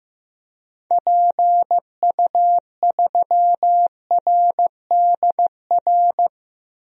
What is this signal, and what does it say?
Morse code, heard as a keyed tone, about 15 wpm, 700 Hz. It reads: PU3RDR
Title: DXCC-100 Call Signs - 6 Characters - 15wpm